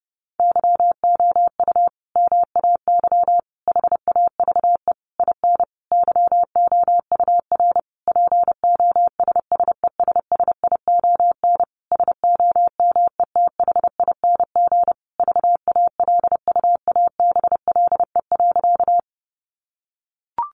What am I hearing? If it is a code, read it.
Morse code, heard as a keyed tone, about 30 wpm, 700 Hz. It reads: YOU MAY HAVE IN YOUR POSSESSION SOMETHING VALUABLE.
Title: Sentences from Top 1400 Words 30wpm